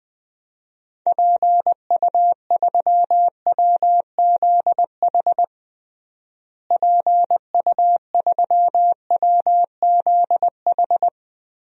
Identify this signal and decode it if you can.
Morse code, heard as a keyed tone, about 20 wpm, 700 Hz. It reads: PU3WZH PU3WZH
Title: DXCC-100 Call Signs - 6 Characters - Repeated 2x - 20wpm